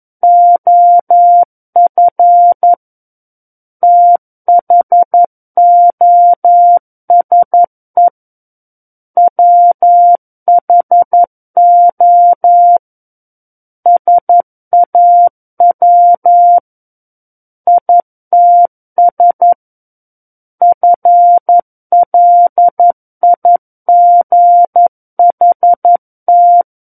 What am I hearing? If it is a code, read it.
Morse code, heard as a keyed tone, about 11 wpm, 700 Hz. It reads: OF THOSE WHO SAW ITS FLIGHT
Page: SKCC - War of the Worlds - 02-Chapter 2 - 11 WPM